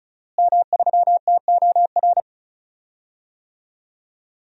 Morse code, heard as a keyed tone, about 36 wpm, 700 Hz. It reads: M3TOR
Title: DXCC-100 Call Signs - 5 Characters - Repeated 3x - 35wpm